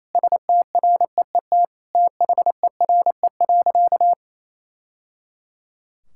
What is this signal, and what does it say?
Morse code, heard as a keyed tone, about 28 wpm, 700 Hz. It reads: STREET THERE.
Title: Sentences from Top 1000 Words - Review Part 5 - 28wpm